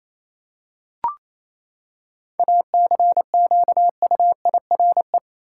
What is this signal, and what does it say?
Morse code, heard as a keyed tone, about 27 wpm, 700 Hz. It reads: ACQUIRE
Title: Seven Letter Words - 28wpm